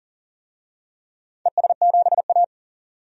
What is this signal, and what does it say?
Morse code, heard as a keyed tone, about 38 wpm, 700 Hz. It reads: ES7A
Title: DXCC-100 Call Signs - 4 Characters - 40wpm